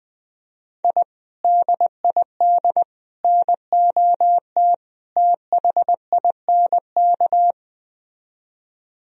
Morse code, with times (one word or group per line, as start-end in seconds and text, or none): 0.8-1.0 I
1.4-2.8 DID
3.2-4.7 NOT
5.2-7.5 THINK